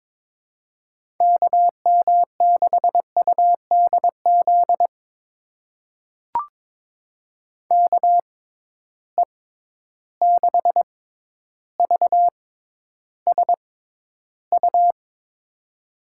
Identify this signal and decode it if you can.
Morse code, heard as a keyed tone, about 22 wpm, spaced as for 11 wpm, 700 Hz. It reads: KM6UDZ KE6VSU
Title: USA Call Signs - 6x Character Spacing - 22wpm